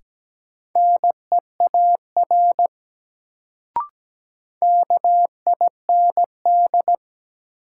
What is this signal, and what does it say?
Morse code, heard as a keyed tone, about 17 wpm, 700 Hz. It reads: NEAR KIND
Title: Sentences from Top 200 Words - New Word Drill - 17wpm